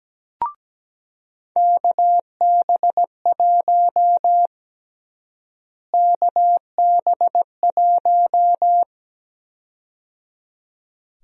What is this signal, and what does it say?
Morse code, heard as a keyed tone, about 17 wpm, 700 Hz. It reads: KB1 KB1
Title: USA Call Sign Prefixes - Repeated 2x - 17wpm